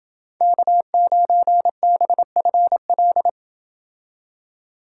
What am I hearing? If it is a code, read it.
Morse code, heard as a keyed tone, about 28 wpm, 700 Hz. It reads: K9BFL